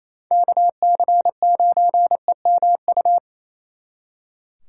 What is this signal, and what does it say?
Morse code, heard as a keyed tone, about 35 wpm, 700 Hz. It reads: KC9EMU
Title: Speed Racing - USA 2x3 Call Signs 28wpm